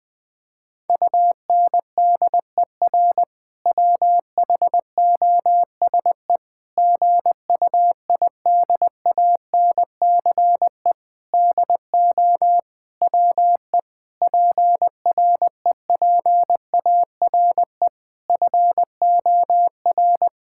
Morse code, heard as a keyed tone, about 20 wpm, 700 Hz. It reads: UNDER WHOSE GUIDANCE DO WE PREPARE FOR
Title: Sentences from Top 1600 Words 20wpm